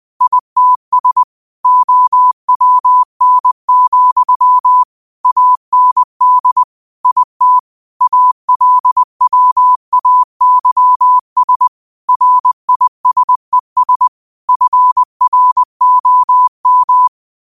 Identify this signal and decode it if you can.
Morse code, heard as a keyed tone, about 20 wpm, 1000 Hz. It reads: ITS OWN, AND IT ALWAYS RISES FROM